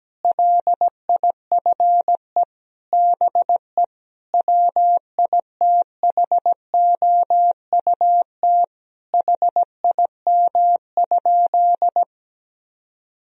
Morse code, heard as a keyed tone, about 17 wpm, 700 Hz. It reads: LIFE BE WITHOUT HIM?
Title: Sentences from Top 1000 Words - Review Part 3 - 17wpm